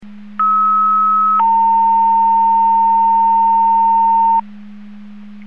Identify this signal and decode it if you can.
Two-tone page, heard as a one-second tone, then a three-second tone, about 1290 then 910 Hz. It is heard from 0.4 to 4.4 s.